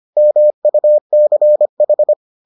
Morse code, heard as a keyed tone, about 25 wpm, 600 Hz. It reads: MUCH